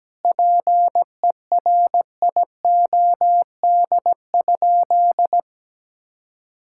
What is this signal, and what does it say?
Morse code, heard as a keyed tone, about 17 wpm, 700 Hz. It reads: PERIOD?
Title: Sentences from Top 1000 Words - Review Part 5 - 17wpm